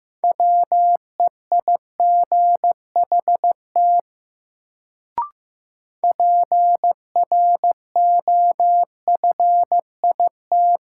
Morse code, heard as a keyed tone, about 15 wpm, 700 Hz. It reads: WEIGHT PROFIT
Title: Sentences from Top 800 Words - New Word Drill - 15wpm